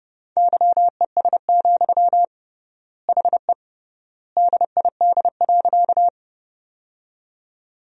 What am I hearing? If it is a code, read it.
Morse code, heard as a keyed tone, about 32 wpm, 700 Hz. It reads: YES, HE DID.
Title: 3x Word-Spacing - Sentences from Top 500 Words - 30wpm